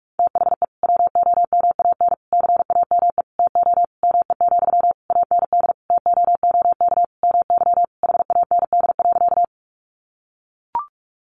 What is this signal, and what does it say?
Morse code, heard as a keyed tone, about 44 wpm, 700 Hz. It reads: THE WOMAN CAME TO ME, AND TOOK MY HAND.
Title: Sentences from Top 1100 Words 45wpm